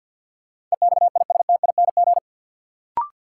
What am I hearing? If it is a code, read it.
Morse code, heard as a keyed tone, about 53 wpm, 700 Hz. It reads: EXISTING
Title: Sentences from Top 1500 Words - New Word Drill - 50wpm